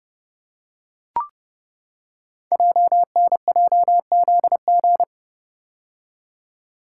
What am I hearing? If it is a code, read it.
Morse code, heard as a keyed tone, about 30 wpm, 700 Hz. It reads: JNJZG
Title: Code Groups - 5 Letters - 30wpm